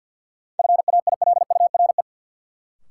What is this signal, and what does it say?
Morse code, heard as a keyed tone, about 51 wpm, 700 Hz. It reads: FAILURE
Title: Sentences from Top 900 Words - New Word Drill - 50wpm